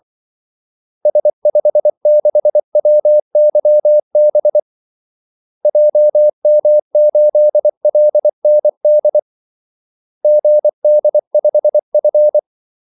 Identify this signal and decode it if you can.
Morse code, heard as a keyed tone, about 24 wpm, 600 Hz. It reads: S56WYB JM8LND GD5F
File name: Callsigns-24wpm